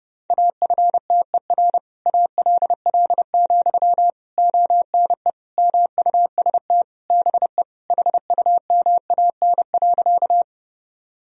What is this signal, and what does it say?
Morse code, heard as a keyed tone, about 32 wpm, 700 Hz. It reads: AFTER ALL, ONE MUST BE HUMAN.
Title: Sentences from Top 1000 Words - Review Part 3 - 30wpm